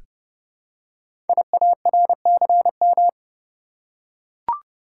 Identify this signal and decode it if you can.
Morse code, heard as a keyed tone, about 28 wpm, 700 Hz. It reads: IARCM
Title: Code Groups - 5 Letters - 30wpm